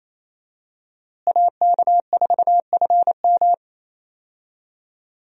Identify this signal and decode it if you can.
Morse code, heard as a keyed tone, about 28 wpm, 700 Hz. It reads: AK4FM
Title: USA Call Signs - 5 Characters - Repeated 2x - 28wpm